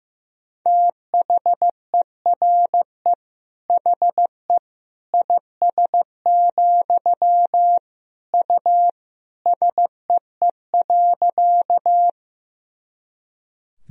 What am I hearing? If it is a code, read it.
Morse code, heard as a keyed tone, about 15 wpm, 700 Hz. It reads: THERE HE IS, U SEE.
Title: Sentences from Top 100 Words and Abbreviations 15wpm